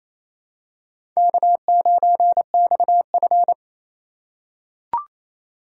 Morse code, heard as a keyed tone, about 28 wpm, 700 Hz. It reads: K9XF
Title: USA Call Signs - 4 Characters - 4x Character Spacing - 28wpm